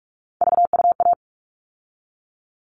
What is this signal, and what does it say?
Morse code, heard as a keyed tone, about 44 wpm, 700 Hz. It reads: VUA
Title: All Cut Numbers - 3-Digit 45wpm